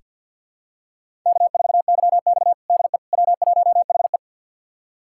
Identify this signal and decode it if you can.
Morse code, heard as a keyed tone, about 46 wpm, 700 Hz. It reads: K4YX DE W1HE
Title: USA Callsign DE Callsign - 4 Characters - 50wpm